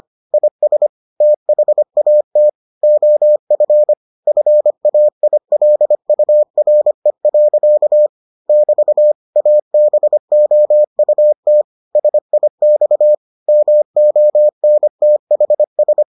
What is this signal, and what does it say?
Morse code, heard as a keyed tone, about 25 wpm, 600 Hz. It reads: IS THAT OF FAILURE. <BT> ABOUT SIX MONTHS